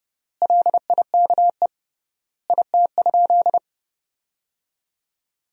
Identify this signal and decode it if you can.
Morse code, heard as a keyed tone, about 30 wpm, 700 Hz. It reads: LIKE IT?
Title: 3x Word-Spacing - Sentences from Top 300 Words - 30wpm